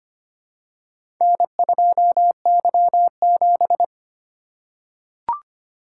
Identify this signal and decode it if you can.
Morse code, heard as a keyed tone, about 25 wpm, 700 Hz. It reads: N2Y7
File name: practice / 25wpm / Four Letter-Number Combinations - 25wpm